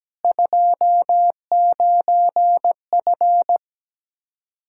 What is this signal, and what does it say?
Morse code, heard as a keyed tone, about 17 wpm, 700 Hz. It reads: 29F
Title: Practice QSO - Exchange 2 - Info - 17wpm